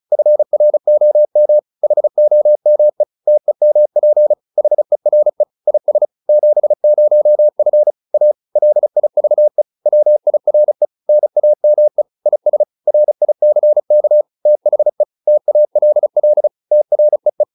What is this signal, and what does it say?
Morse code, heard as a keyed tone, about 34 wpm, 600 Hz. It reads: FROM HOME TEMP HERE IS 70F A LIVE WIRE NAME IS RICK THE TALL TREE